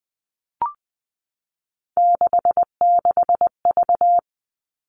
Morse code, heard as a keyed tone, about 20 wpm, 700 Hz. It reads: 66V
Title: Three Challenging Letter-Number Combinations 20wpm